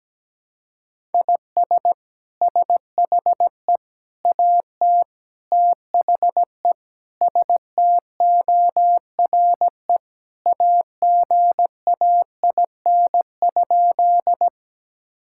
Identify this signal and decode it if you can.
Morse code, heard as a keyed tone, about 17 wpm, 700 Hz. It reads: IS SHE AT THE STORE AGAIN?
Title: Sentences from Top 1000 Words - Review Part 5 - 17wpm